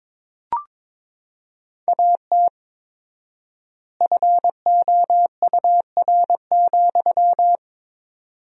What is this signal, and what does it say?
Morse code, heard as a keyed tone, about 21 wpm, 700 Hz. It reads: AT FOUR,
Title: 4x Word-Spacing - Sentences from Top 300 Words - 22wpm